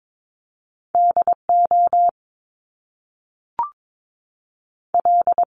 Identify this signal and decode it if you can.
Morse code, heard as a keyed tone, about 22 wpm, spaced as for 3 wpm, 700 Hz. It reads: D OL
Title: Top Two Letter Combinations in Words 22wpm